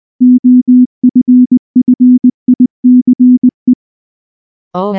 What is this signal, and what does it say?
Morse code, heard as a keyed tone, about 19 wpm, 250 Hz. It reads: OFFICE
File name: morse / morse-training-20WPM-5